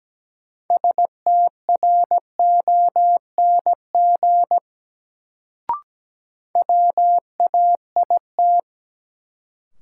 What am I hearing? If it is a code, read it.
Morse code, heard as a keyed tone, about 17 wpm, 700 Hz. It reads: STRONG WAIT
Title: Sentences from Top 500 Words - New Word Drill - 17wpm